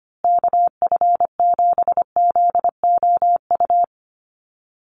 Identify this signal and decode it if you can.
Morse code, heard as a keyed tone, about 25 wpm, 700 Hz. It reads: KF7ZOU
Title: USA Call Signs Repeated 3x 25wpm